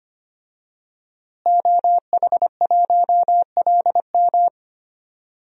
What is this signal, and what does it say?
Morse code, heard as a keyed tone, about 25 wpm, 700 Hz. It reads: OH1LM
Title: DXCC-100 Call Signs - 5 Characters - Repeated 2x - 25wpm